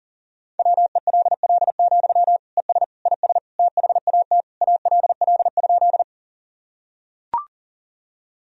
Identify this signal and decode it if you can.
Morse code, heard as a keyed tone, about 40 wpm, 700 Hz. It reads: WELL, ES IS THAT ALL?
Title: Sentences from Top 200 Words and Abbreviations 40wpm